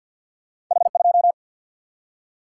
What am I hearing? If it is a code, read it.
Morse code, heard as a keyed tone, about 46 wpm, 700 Hz. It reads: H 2
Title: Single-Letter Single-Number 50wpm